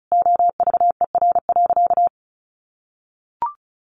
Morse code, heard as a keyed tone, about 34 wpm, 700 Hz. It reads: OVER.
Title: Additional Easy Phrases 35wpm